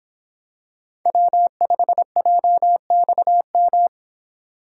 Morse code, heard as a keyed tone, about 26 wpm, 700 Hz. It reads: W5JXM